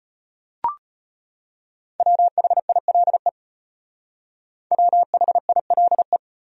Morse code, heard as a keyed tone, about 38 wpm, 700 Hz. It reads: WHILE WHILE